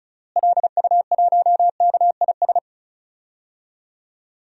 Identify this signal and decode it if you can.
Morse code, heard as a keyed tone, about 34 wpm, 700 Hz. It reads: LU1KIS
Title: DXCC-100 Call Signs - 6 Characters - Repeated 2x - 35wpm